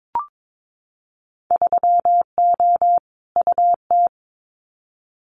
Morse code, heard as a keyed tone, about 22 wpm, 700 Hz. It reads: 3O UT
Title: Winter Field Day Exchanges 22wpm